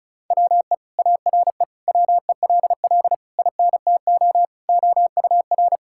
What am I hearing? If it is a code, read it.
Morse code, heard as a keyed tone, about 34 wpm, 700 Hz. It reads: WE ARE WELL INTO OUR
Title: Sentences from Top 1000 Words - Review Part 5 - 35wpm